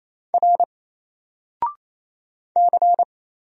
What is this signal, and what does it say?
Morse code, heard as a keyed tone, about 27 wpm, 700 Hz. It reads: R C
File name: Single-Letter-Number-28wpm